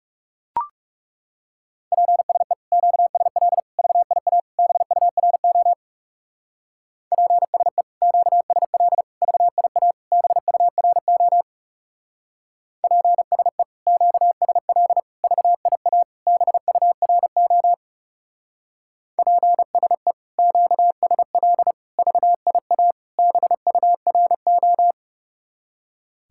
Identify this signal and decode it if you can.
Morse code, heard as a keyed tone, about 35 wpm, 700 Hz. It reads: PSE QSL VIA BURO PSE QSL VIA BURO PSE QSL VIA BURO PSE QSL VIA BURO
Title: Speed-Racing Common QSO Phrases - 30wpm